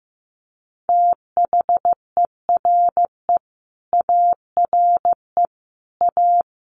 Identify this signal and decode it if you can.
Morse code, heard as a keyed tone, about 14 wpm, 700 Hz. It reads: THERE ARE A
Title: Sets of 5 Words 15wpm